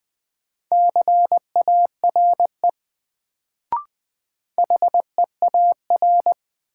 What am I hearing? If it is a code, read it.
Morse code, heard as a keyed tone, about 20 wpm, 700 Hz. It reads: CARE HEAR
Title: Sentences from Top 300 Words - New Word Drill - 20wpm